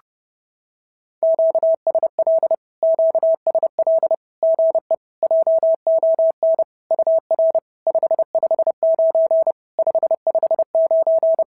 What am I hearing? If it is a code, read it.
Morse code, heard as a keyed tone, about 30 wpm, 650 Hz. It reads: QSL QSL GE JON UR 559 559
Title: SOTA QSO Practice 30wpm